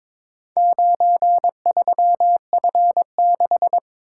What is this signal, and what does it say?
Morse code, heard as a keyed tone, about 23 wpm, 700 Hz. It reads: 93F6
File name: practice / 22wpm / Four Letter-Number Combinations - 22wpm